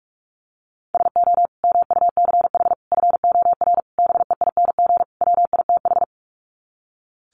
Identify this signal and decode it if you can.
Morse code, heard as a keyed tone, about 44 wpm, 700 Hz. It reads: SO MUCH FOR BEING WITH
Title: Sets of 5 Words Encore 45wpm